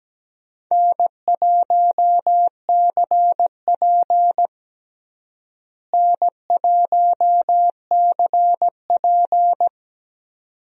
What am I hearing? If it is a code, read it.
Morse code, heard as a keyed tone, about 17 wpm, 700 Hz. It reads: N1CP N1CP
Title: USA Call Signs - 4 Characters - Repeated 2x - 17wpm